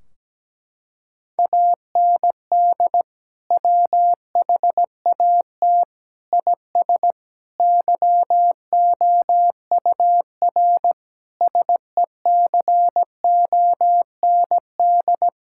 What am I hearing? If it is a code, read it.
Morse code, heard as a keyed tone, about 17 wpm, 700 Hz. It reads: AND WHAT IS YOUR SECOND
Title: Sentences from Top 1600 Words 17wpm